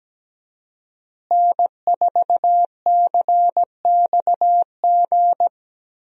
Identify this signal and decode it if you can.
Morse code, heard as a keyed tone, about 17 wpm, 700 Hz. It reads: N4CXG
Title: USA Call Signs - 5 Characters - Repeated 3x - 17wpm